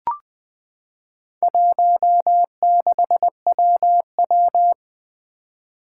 Morse code, heard as a keyed tone, about 20 wpm, 700 Hz. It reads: 16WW
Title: Four Challenging Letter-Number Combinations 20wpm